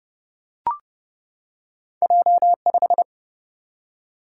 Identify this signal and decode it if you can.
Morse code, heard as a keyed tone, about 30 wpm, 700 Hz. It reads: J5
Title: Single-Letter Single-Number 30wpm